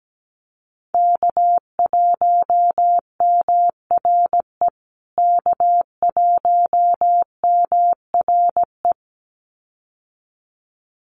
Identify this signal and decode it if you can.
Morse code, heard as a keyed tone, about 17 wpm, 700 Hz. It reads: K1MRE K1MRE
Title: USA Call Signs Repeated 2x 17wpm